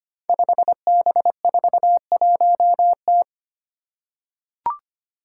Morse code, heard as a keyed tone, about 25 wpm, 700 Hz. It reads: 5B41T
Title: Five Challenging Letter-Number Combinations 25wpm